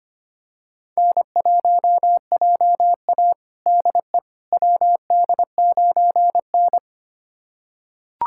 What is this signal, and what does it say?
Morse code, heard as a keyed tone, about 25 wpm, 700 Hz. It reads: N1JA DE WD9N
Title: USA Callsign DE Callsign - 4 Characters - 25wpm